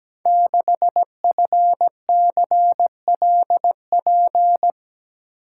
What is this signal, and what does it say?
Morse code, heard as a keyed tone, about 17 wpm, 700 Hz. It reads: 6FCLP
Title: Code Groups - 5 Letter:Numbers - 17wpm